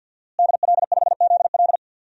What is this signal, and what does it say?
Morse code, heard as a keyed tone, about 52 wpm, 700 Hz. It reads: DL5ZL
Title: DXCC-100 Call Signs - 5 Characters - Repeated 3x - 50wpm